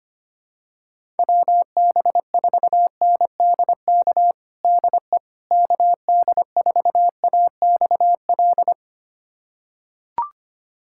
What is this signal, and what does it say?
Morse code, heard as a keyed tone, about 25 wpm, 700 Hz. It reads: WB4NDK DE KD4AXL
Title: USA Callsign DE Callsign - 6 Characters - 25wpm